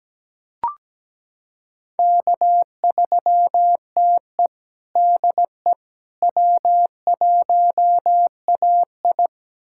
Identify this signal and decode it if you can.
Morse code, heard as a keyed tone, about 17 wpm, 700 Hz. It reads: K3TE DE W1AI
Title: USA Callsign DE Callsign - 4 Characters - 17wpm